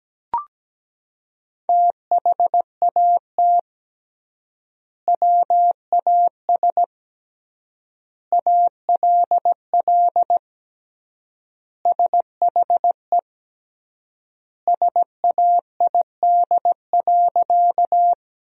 Morse code, heard as a keyed tone, about 17 wpm, 700 Hz. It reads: THAT WAS ALL SHE SAID.
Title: 3x Word-Spacing - Sentences from Top 100 Words - 17wpm